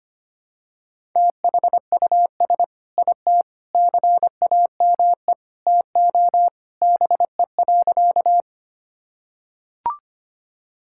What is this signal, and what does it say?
Morse code, heard as a keyed tone, about 25 wpm, 700 Hz. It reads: THUS IT CAME TO BE.
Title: Sentences from Top 1000 Words - Review Part 5 - 25wpm